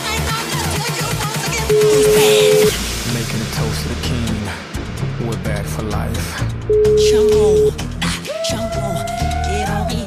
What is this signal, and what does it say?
Ringback tone (Europe, 425 Hz), with speech.